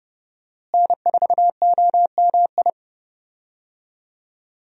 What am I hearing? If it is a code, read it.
Morse code, heard as a keyed tone, about 30 wpm, 700 Hz. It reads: N4OMI
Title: USA Call Signs - 5 Characters - Repeated 2x - 30wpm